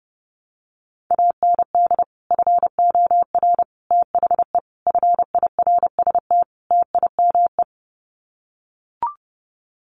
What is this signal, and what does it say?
Morse code, heard as a keyed tone, about 30 wpm, 700 Hz. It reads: AND FOR THE FIRST TIME
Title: Sets of 5 Words 30wpm